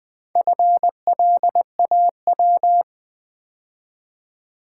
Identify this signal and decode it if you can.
Morse code, heard as a keyed tone, about 20 wpm, 700 Hz. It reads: FLAW
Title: Build A Word - 2 Letters At A Time - 4-Letter Words - 20wpm